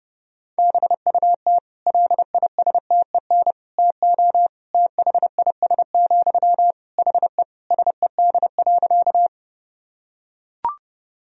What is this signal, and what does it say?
Morse code, heard as a keyed tone, about 30 wpm, 700 Hz. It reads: BUT LISTEN TO THIS, HE SED.
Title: Sentences from Top 400 Words and Abbreviations 30wpm